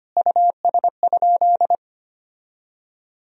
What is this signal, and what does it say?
Morse code, heard as a keyed tone, about 25 wpm, 700 Hz. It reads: US?
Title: Sentences from Top 1000 Words - Review Part 3 - 25wpm